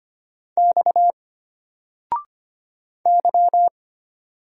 Morse code, heard as a keyed tone, about 25 wpm, 700 Hz. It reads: X Y
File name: Single-Letter-Number-25wpm